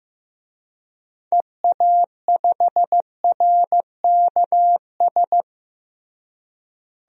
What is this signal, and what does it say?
Morse code, heard as a keyed tone, about 15 wpm, 700 Hz. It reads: EA5RKS
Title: DXCC-100 Call Signs - 6 Characters - Repeated 2x - 15wpm